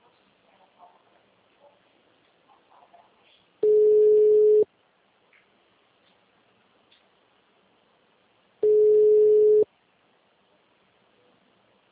Ringback tone (Europe, 425 Hz).
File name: out- -2009- - - .39625